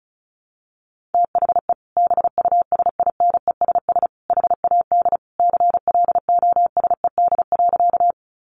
Morse code, heard as a keyed tone, about 34 wpm, 700 Hz. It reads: THE BUSINESS HAD CLOSED.
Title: Sentences from Top 1400 Words 35wpm